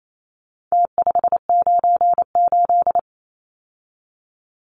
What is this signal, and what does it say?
Morse code, heard as a keyed tone, about 28 wpm, 700 Hz. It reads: T598
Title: Numbers - Contest Style - 4-Digit 28wpm